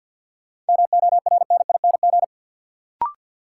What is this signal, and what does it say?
Morse code, heard as a keyed tone, about 53 wpm, 700 Hz. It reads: MORNING
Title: Sentences from Top 400 Words - New Word Drill - 50wpm